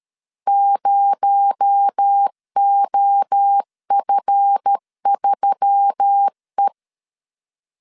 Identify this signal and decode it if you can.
Morse code, heard as a keyed tone, about 13 wpm, 800 Hz. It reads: 0OF3E
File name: CW / DET / 1 9 1500C S A 12wpm 3800